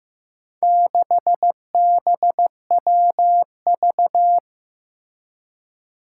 Morse code, heard as a keyed tone, about 15 wpm, 700 Hz. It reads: 6BWV
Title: Four Challenging Letter-Number Combinations 15wpm